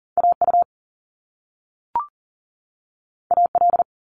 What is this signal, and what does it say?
Morse code, heard as a keyed tone, about 40 wpm, 700 Hz. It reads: AU AL
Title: Top Two Letter Combinations in Words 40wpm